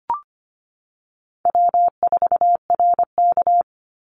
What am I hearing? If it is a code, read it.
Morse code, heard as a keyed tone, about 25 wpm, 700 Hz. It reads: W4RK
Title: NA SOTA Chaser Callsigns 25wpm